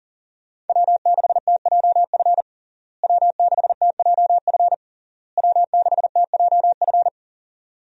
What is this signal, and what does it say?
Morse code, heard as a keyed tone, about 40 wpm, 700 Hz. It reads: W6TJF W6TJF W6TJF
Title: USA Call Signs - 5 Characters - Repeated 3x - 40wpm